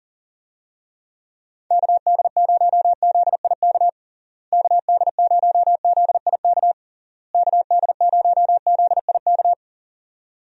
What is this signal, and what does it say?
Morse code, heard as a keyed tone, about 40 wpm, 700 Hz. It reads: KD0ZIK KD0ZIK KD0ZIK
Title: USA Call Signs - 6 Characters - Repeated 3x - 40wpm